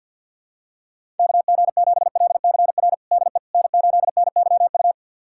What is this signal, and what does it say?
Morse code, heard as a keyed tone, about 52 wpm, 690 Hz. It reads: KG7LCR DE N8NYU
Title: USA Callsign DE Callsign 50wpm